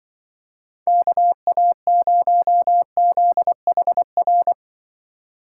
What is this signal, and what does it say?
Morse code, heard as a keyed tone, about 24 wpm, 700 Hz. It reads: KA0ZHR
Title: Speed Racing - USA Call Signs 20wpm